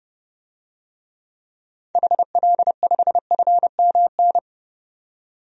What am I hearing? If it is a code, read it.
Morse code, heard as a keyed tone, about 30 wpm, 700 Hz. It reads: HL5FMN
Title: DXCC-100 Call Signs - 6 Characters - 30wpm